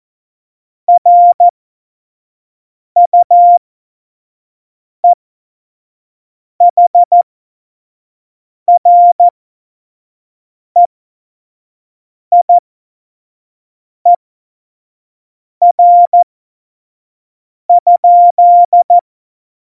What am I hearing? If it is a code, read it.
Morse code, heard as a keyed tone, about 14 wpm, spaced as for 5 wpm, 700 Hz. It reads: RUEHREIER?
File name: Morsekurs / Witze / Witz 17. September 2019, 25 F70